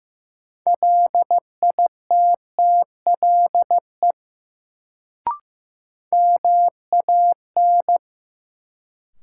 Morse code, heard as a keyed tone, about 15 wpm, 700 Hz. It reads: LITTLE MAN
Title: Sentences from Top 200 Words - New Word Drill - 15wpm